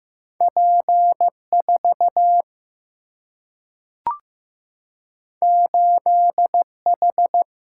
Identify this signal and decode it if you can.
Morse code, heard as a keyed tone, about 15 wpm, 700 Hz. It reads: P 4 8 H
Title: Single-Letter Single-Number 15wpm